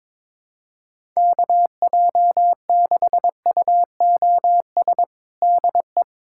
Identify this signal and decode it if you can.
Morse code, heard as a keyed tone, about 22 wpm, 700 Hz. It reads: KJ6UOS DE
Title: USA Callsign DE Callsign - 6 Characters - 22wpm